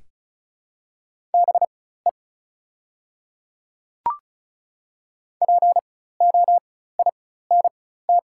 Morse code, heard as a keyed tone, about 36 wpm, spaced as for 16 wpm, 700 Hz. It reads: BE POINT
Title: 4x Character Spacing - Top 200 Words - 35wpm